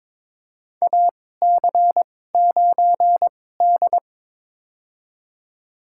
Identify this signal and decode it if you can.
Morse code, heard as a keyed tone, about 22 wpm, spaced as for 16 wpm, 700 Hz. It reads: AC9D
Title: USA Call Signs - 2x Character Spacing - 22wpm